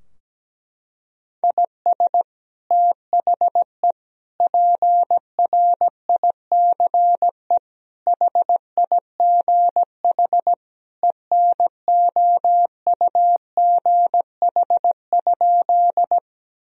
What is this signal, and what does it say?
Morse code, heard as a keyed tone, about 17 wpm, 700 Hz. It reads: IS THE PRICE HIGH ENOUGH?
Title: Sentences from Top 1000 Words - Review Part 4 - 17wpm